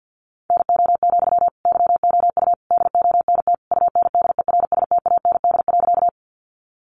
Morse code, heard as a keyed tone, about 48 wpm, 700 Hz. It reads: NO, YOU DONT UNDERSTAND.
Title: Sentences from Top 600 Words 50wpm